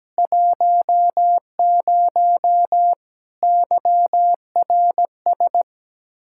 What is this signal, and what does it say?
Morse code, heard as a keyed tone, about 17 wpm, 700 Hz. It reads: 10 YRS
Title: Practice QSO - Exchange 3 - More Info - 17wpm